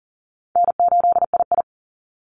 Morse code, heard as a keyed tone, about 40 wpm, 700 Hz. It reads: N8II
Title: Call Signs - 40wpm